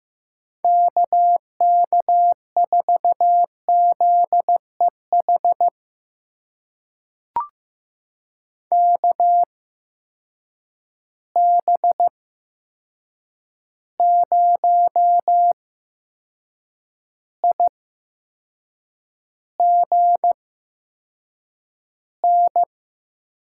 Morse code, heard as a keyed tone, about 15 wpm, spaced as for 6 wpm, 700 Hz. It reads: KK4ZEH KB0IGN
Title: USA Call Signs - 6 Characters - 8x Character Spacing - 15wpm